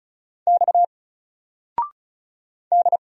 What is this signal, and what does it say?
Morse code, heard as a keyed tone, about 36 wpm, 700 Hz. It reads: X D
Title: Single Letter-Number 35wpm